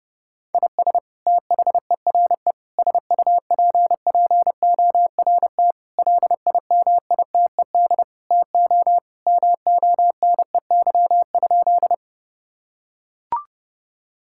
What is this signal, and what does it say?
Morse code, heard as a keyed tone, about 32 wpm, 700 Hz. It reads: IS THERE SUPPORT LIMITED TO MONEY?
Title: Sentences from Top 1000 Words - Review Part 6 - 30wpm